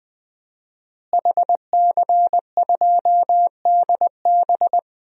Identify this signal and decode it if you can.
Morse code, heard as a keyed tone, about 20 wpm, 700 Hz. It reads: HC2DB
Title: DXCC-100 Call Signs - 5 Characters - Repeated 2x - 20wpm